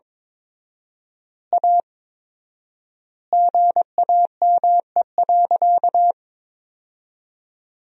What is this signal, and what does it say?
Morse code, heard as a keyed tone, about 22 wpm, 700 Hz. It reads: A GAME.
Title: 4x Word-Spacing - Sentences from Top 500 Words - 22wpm